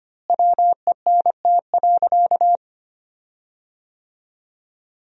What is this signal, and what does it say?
Morse code, heard as a keyed tone, about 25 wpm, 700 Hz. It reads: WENT.
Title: 4x Word-Spacing - Sentences from Top 200 Words - 25wpm